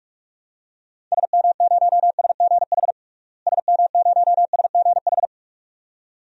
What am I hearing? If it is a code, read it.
Morse code, heard as a keyed tone, about 46 wpm, 700 Hz. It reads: SM0SGH SM0SGH
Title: DXCC-100 Call Signs - 6 Characters - Repeated 2x - 45wpm